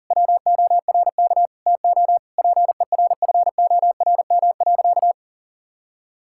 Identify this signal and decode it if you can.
Morse code, heard as a keyed tone, about 40 wpm, 700 Hz. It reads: WORK TO PERFORM.
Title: Sentences from Top 1000 Words - Review Part 5 - 40wpm